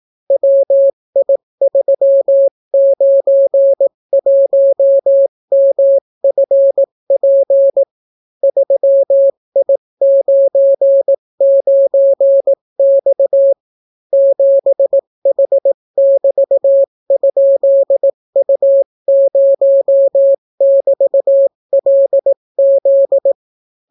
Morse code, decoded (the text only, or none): WI391MFP 3I99X 7H<BT>?U0<BT>LZ